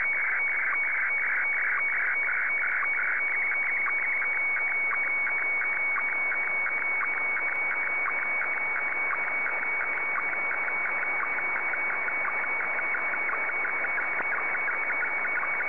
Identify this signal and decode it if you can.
SSTV (slow-scan television), Scottie DX.